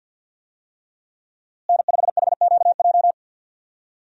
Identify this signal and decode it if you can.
Morse code, heard as a keyed tone, about 51 wpm, 700 Hz. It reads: N5HQJ
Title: USA Call Signs - 5 Characters - Repeated 3x - 50wpm